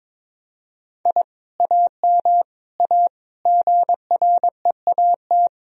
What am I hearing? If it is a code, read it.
Morse code, heard as a keyed tone, about 22 wpm, 700 Hz. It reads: I AM A GREAT
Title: Sentences from Top 1000 Words - Review Part 4 - 22wpm